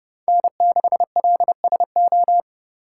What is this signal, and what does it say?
Morse code, heard as a keyed tone, about 36 wpm, 700 Hz. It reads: N6LSO
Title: Speed Racing - USA 1x3 Call Signs 30wpm